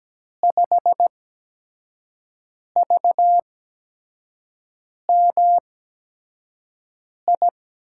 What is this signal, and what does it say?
Morse code, heard as a keyed tone, about 17 wpm, spaced as for 5 wpm, 700 Hz. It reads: 5VMI
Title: USA Call Signs - 6 Characters - 8x Character Spacing - 17wpm